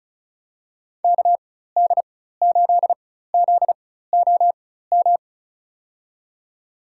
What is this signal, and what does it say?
Morse code, heard as a keyed tone, about 34 wpm, spaced as for 16 wpm, 700 Hz. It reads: KD8ZOM
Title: USA Call Signs - 6 Characters - 4x Character Spacing - 35wpm